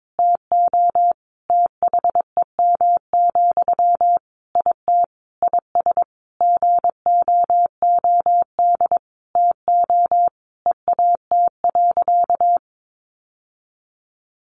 Morse code, heard as a keyed tone, about 22 wpm, 700 Hz. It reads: TO THEM, IT IS GOOD TO EAT.
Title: Sentences from Top 800 Words 22wpm